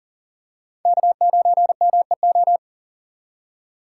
Morse code, heard as a keyed tone, about 37 wpm, 700 Hz. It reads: K9MEO
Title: USA Call Signs - 5 Characters - 2x Character Spacing - 40wpm